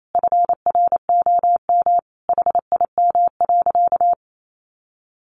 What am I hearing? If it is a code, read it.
Morse code, heard as a keyed tone, about 28 wpm, 700 Hz. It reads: FROM HIM.
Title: Sentences from Top 1200 Words 28wpm